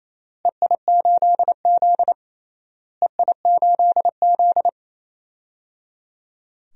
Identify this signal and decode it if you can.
Morse code, heard as a keyed tone, about 28 wpm, 700 Hz. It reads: EI8Z EI8Z
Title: DXCC-100 Call Signs - 4 Characters - Repeated 3x - 28wpm